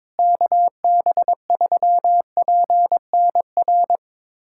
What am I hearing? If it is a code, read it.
Morse code, heard as a keyed tone, about 22 wpm, 700 Hz. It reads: KB3PNR
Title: USA Callsign DE Callsign - 6 Characters - 22wpm